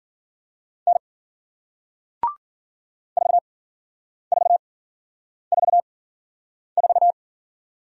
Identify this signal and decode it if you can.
Morse code, heard as a keyed tone, about 53 wpm, spaced as for 9 wpm, 700 Hz. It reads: N 4444